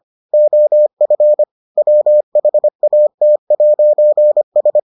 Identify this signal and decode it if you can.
Morse code, heard as a keyed tone, about 25 wpm, 600 Hz. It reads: OF WHAT'S